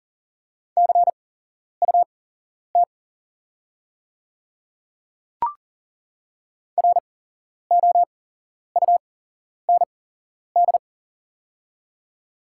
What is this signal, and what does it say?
Morse code, heard as a keyed tone, about 40 wpm, spaced as for 11 wpm, 690 Hz. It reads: CUT ROUND
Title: 8x Character Spacing - Top 500 Words - 40wpm